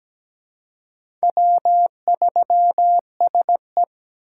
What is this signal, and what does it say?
Morse code, heard as a keyed tone, about 17 wpm, 700 Hz. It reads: W3SE
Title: USA Call Signs - 4 Characters - 2x Character Spacing - 17wpm